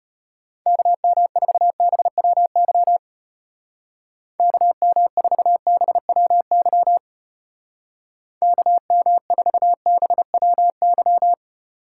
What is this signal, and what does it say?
Morse code, heard as a keyed tone, about 30 wpm, 700 Hz. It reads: KM4BWY KM4BWY KM4BWY
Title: Speed Racing - USA 2x3 Call Signs 25wpm